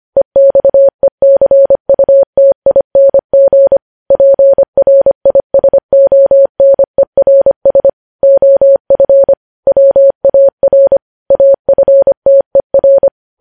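Morse code, heard as a keyed tone, about 25 wpm, 550 Hz. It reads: EXECUTING PRISONERS OF WAR AFTER
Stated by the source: Morse news headlines sent at 25 wpm.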